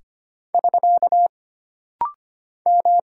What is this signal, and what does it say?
Morse code, heard as a keyed tone, about 25 wpm, 700 Hz. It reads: <SK> M
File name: Single Character (Letter:Number:Punctuation:Prosign) - Rapid-Fire - 25wpm